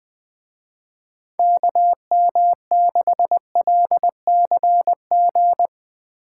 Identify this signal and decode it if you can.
Morse code, heard as a keyed tone, about 20 wpm, 700 Hz. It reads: KM6LCG